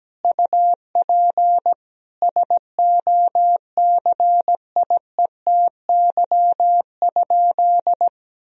Morse code, heard as a keyed tone, about 17 wpm, 700 Hz. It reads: UP SOCIETY?
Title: Sentences from Top 1000 Words - Review Part 4 - 17wpm